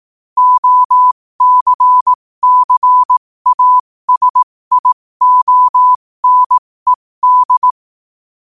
Morse code, heard as a keyed tone, about 18 wpm, spaced as for 15 wpm, 1000 Hz. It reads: OCCASIONED
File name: qotdfiles / 15wpm